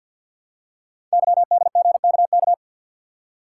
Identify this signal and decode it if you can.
Morse code, heard as a keyed tone, about 48 wpm, 700 Hz. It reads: YDGKK
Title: Code Groups - 5 Letters - 50wpm